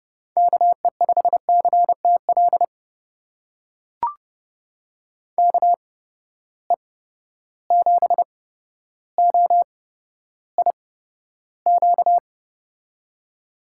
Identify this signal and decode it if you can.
Morse code, heard as a keyed tone, about 30 wpm, spaced as for 13 wpm, 700 Hz. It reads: KE5CTL KE7OIQ
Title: USA Call Signs - 6 Characters - 8x Character Spacing - 30wpm